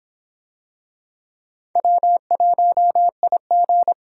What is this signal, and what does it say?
Morse code, heard as a keyed tone, about 24 wpm, 700 Hz. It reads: W1IG